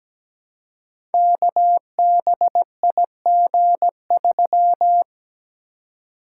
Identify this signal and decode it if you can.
Morse code, heard as a keyed tone, about 17 wpm, 700 Hz. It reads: KBIG3
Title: Code Groups - 5 Letter:Numbers - 17wpm